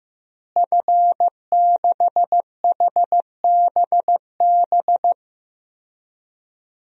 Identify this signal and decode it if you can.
Morse code, heard as a keyed tone, about 15 wpm, 700 Hz. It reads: F6HBB
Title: DXCC-100 Call Signs - 5 Characters - Repeated 2x - 15wpm